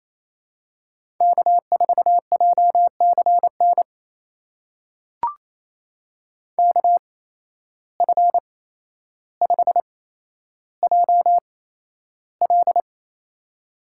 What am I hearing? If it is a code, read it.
Morse code, heard as a keyed tone, about 27 wpm, spaced as for 12 wpm, 700 Hz. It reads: K4JCN KF5JL
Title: USA Call Signs - 5 Characters - 8x Character Spacing - 28wpm